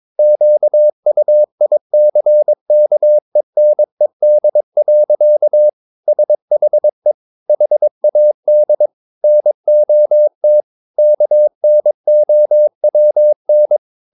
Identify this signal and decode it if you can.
Morse code, heard as a keyed tone, about 22 wpm, 600 Hz. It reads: QUICKENED. SHE HAD NOT KNOWN
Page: Romanen Never af Ken Follet på engelsk.